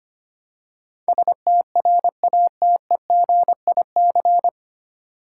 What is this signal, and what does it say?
Morse code, heard as a keyed tone, about 25 wpm, 700 Hz. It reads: STRATEGIC
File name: practice / 25wpm / 9 Letter Words - First 3 Letters Given - 25wpm